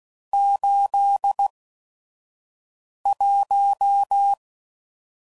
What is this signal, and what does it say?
Morse code, heard as a keyed tone, about 16 wpm, spaced as for 5 wpm, 800 Hz. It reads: 81
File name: CW Code Groups @5WPM / Weather Sets / Weather 5 12